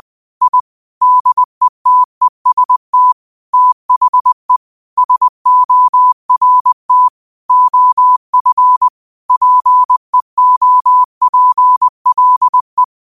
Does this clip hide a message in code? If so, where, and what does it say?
Yes: Morse code from 0.4 to 12.8 s, heard as a keyed tone, about 20 wpm, 1000 Hz. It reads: I DETEST THE SORT OF PEOPLE